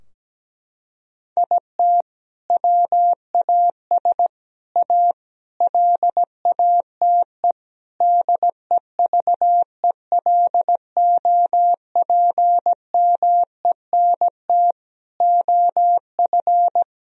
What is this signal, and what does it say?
Morse code, heard as a keyed tone, about 17 wpm, 700 Hz. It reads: IT WAS A LATE DEVELOPMENT OF
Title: Sentences from Top 1000 Words - Review Part 4 - 17wpm